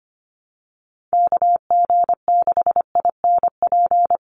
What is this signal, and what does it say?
Morse code, heard as a keyed tone, about 25 wpm, 700 Hz. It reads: KG6INP
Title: USA Call Signs 25wpm